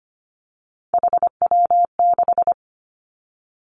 Morse code, heard as a keyed tone, about 25 wpm, 700 Hz. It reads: HW6
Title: Three Challenging Letter-Number Combinations 25wpm